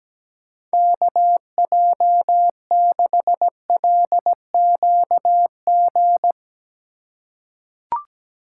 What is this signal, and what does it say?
Morse code, heard as a keyed tone, about 17 wpm, 700 Hz. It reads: KJ6LQG
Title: USA Call Signs - 4x Character Spacing - 17wpm